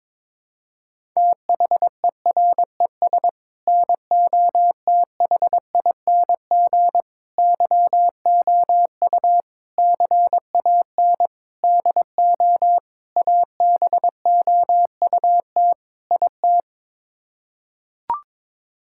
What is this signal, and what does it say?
Morse code, heard as a keyed tone, about 22 wpm, 700 Hz. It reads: THERES NOTHING YOU CAN DO ABOUT IT
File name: practice / 22wpm / Sets of 7 Words - 22wpm